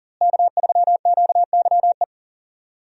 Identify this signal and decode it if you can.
Morse code, heard as a keyed tone, about 40 wpm, 700 Hz. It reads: K3QYE
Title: USA Call Signs - 5 Characters - Repeated 3x - 40wpm